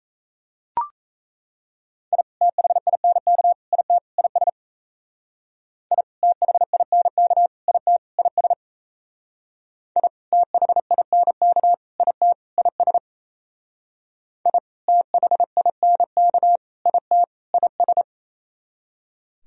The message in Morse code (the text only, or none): I THINK IT IS I THINK IT IS I THINK IT IS I THINK IT IS